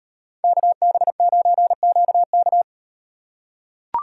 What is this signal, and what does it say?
Morse code, heard as a keyed tone, about 38 wpm, 700 Hz. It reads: KB9QK